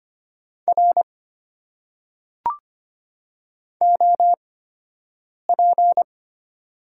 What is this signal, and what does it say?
Morse code, heard as a keyed tone, about 25 wpm, spaced as for 7 wpm, 700 Hz. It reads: R OP
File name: Top 90 QSO Elements - 8x Character Spacing - 25wpm